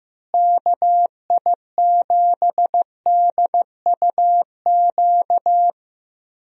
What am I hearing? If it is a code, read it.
Morse code, heard as a keyed tone, about 15 wpm, 700 Hz. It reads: KI7DUQ
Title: USA Call Signs - 6 Characters - Repeated 3x - 15wpm